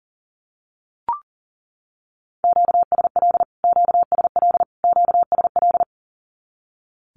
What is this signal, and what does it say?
Morse code, heard as a keyed tone, about 40 wpm, 700 Hz. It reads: QSL QSL QSL
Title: Q-Codes Repeated 3x 40wpm